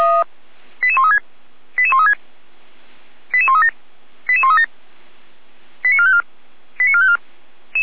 Five-tone selective call (ZVEI), heard as quick runs of five tones, tones of about 80 ms. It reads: 88136 88136 88137 88137 89454 89454